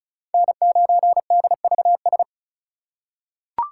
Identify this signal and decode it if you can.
Morse code, heard as a keyed tone, about 36 wpm, 700 Hz. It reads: N9DVS
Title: USA Call Signs - 5 Characters - 4x Character Spacing - 35wpm